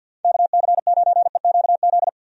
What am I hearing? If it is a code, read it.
Morse code, heard as a keyed tone, about 46 wpm, 700 Hz. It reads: KC9EQZ
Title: USA Call Signs - 6 Characters - 50wpm